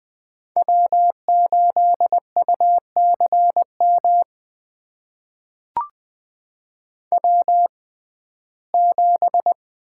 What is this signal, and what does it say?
Morse code, heard as a keyed tone, about 20 wpm, spaced as for 14 wpm, 700 Hz. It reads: W8UCM W7
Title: USA Call Signs - 5 Characters - 6x Character Spacing - 20wpm